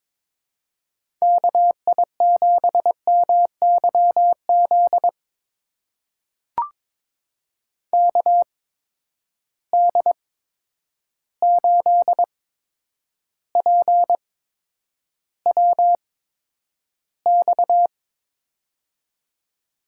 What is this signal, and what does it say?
Morse code, heard as a keyed tone, about 23 wpm, spaced as for 10 wpm, 700 Hz. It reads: KI7MYZ KD8PWX
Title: USA Call Signs - 8x Character Spacing - 22wpm